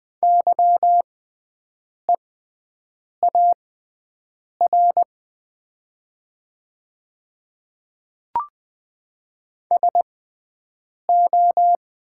Morse code, heard as a keyed tone, about 20 wpm, spaced as for 7 wpm, 700 Hz. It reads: YEAR SO
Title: 6x Character Spacing - Top 200 Words - 20wpm